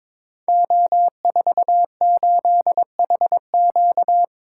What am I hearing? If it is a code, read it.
Morse code, heard as a keyed tone, about 22 wpm, 700 Hz. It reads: O48HQ
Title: Code Groups - 5 Letter:Numbers - 22wpm